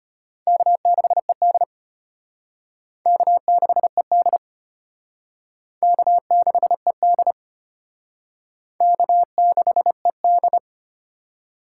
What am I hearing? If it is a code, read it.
Morse code, heard as a keyed tone, about 32 wpm, 700 Hz. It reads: K6ED K6ED K6ED K6ED